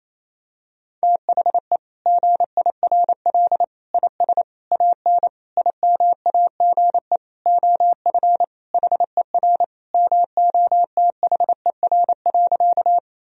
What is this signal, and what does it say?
Morse code, heard as a keyed tone, about 28 wpm, 700 Hz. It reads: THE GIRL IS AN IMAGE OF HER MOTHER.
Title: Sentences from Top 1000 Words - Review Part 4 - 28wpm